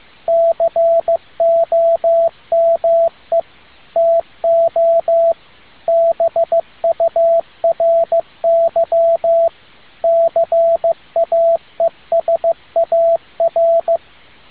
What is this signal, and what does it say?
Morse code, heard as a keyed tone, about 15 wpm, 670 Hz, with radio noise. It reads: COME TO BURY CAESAR